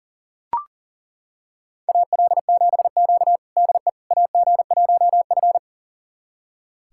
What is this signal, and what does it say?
Morse code, heard as a keyed tone, about 40 wpm, 700 Hz. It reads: AL7Q DE AG1F
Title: USA Callsign DE Callsign - 4 Characters - 40wpm